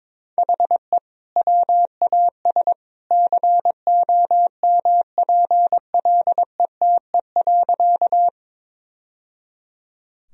Morse code, heard as a keyed tone, about 21 wpm, 700 Hz. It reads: HE WAS COMPLETE.
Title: Sentences from Top 1000 Words - Review Part 2 - 22wpm